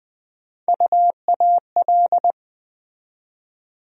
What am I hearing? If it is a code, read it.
Morse code, heard as a keyed tone, about 20 wpm, 700 Hz. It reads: UAL
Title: USA Call Sign Postfixes - 3 Letters 20wpm